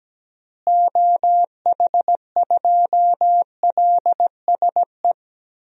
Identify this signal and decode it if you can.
Morse code, heard as a keyed tone, about 17 wpm, 700 Hz. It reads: OH2LSE
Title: DXCC-100 Call Signs - 6 Characters - Repeated 3x - 17wpm